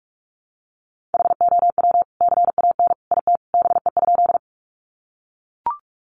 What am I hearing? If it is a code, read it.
Morse code, heard as a keyed tone, about 44 wpm, 700 Hz. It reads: HOW CAN IT BE?
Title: Sentences from Top 100 Words 45wpm